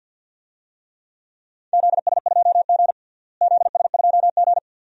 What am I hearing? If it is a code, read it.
Morse code, heard as a keyed tone, about 52 wpm, 700 Hz. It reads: ZS2G ZS2G
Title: DXCC-100 Call Signs - 4 Characters - Repeated 2x - 50wpm